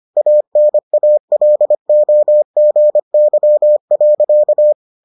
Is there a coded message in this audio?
Yes: Morse code from 0.2 to 4.7 s, heard as a keyed tone, about 25 wpm, 600 Hz. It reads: ANALOGY.